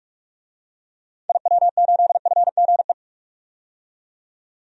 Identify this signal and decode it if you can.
Morse code, heard as a keyed tone, about 46 wpm, 700 Hz. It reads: IW8FGE
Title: DXCC-100 Call Signs - 6 Characters - 45wpm